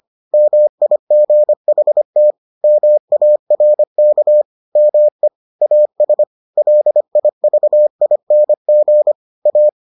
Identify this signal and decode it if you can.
Morse code, heard as a keyed tone, about 25 wpm, 600 Hz. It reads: MIGHT MARK ME AS LIVING A